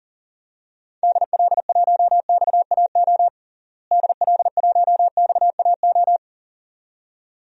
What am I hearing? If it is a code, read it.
Morse code, heard as a keyed tone, about 40 wpm, 700 Hz. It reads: DL1XAO DL1XAO
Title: DXCC-100 Call Signs - 6 Characters - Repeated 2x - 40wpm